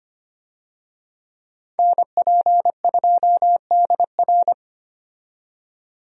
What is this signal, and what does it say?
Morse code, heard as a keyed tone, about 25 wpm, 700 Hz. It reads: NP2DR
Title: USA Call Signs - 5 Characters - 25wpm